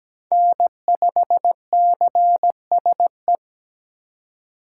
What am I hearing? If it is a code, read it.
Morse code, heard as a keyed tone, about 17 wpm, 700 Hz. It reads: N5CSE
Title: USA Call Signs - 5 Characters - 3x Character Spacing - 17wpm